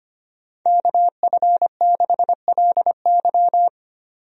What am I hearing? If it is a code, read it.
Morse code, heard as a keyed tone, about 25 wpm, 700 Hz. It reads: KF6LY
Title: USA Call Signs - 5 Characters - 25wpm